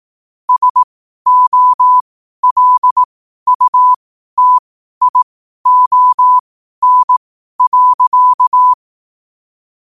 Morse code, heard as a keyed tone, about 18 wpm, spaced as for 12 wpm, 1000 Hz. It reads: SOLUTION.
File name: qotdfiles / 12wpm